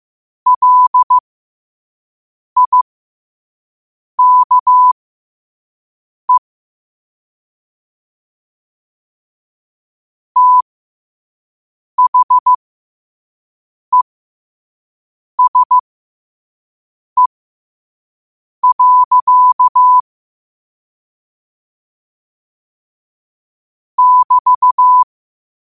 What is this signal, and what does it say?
Morse code, heard as a keyed tone, about 15 wpm, spaced as for 5 wpm, 1000 Hz. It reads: LIKE THESE. <BT>
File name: 5wpm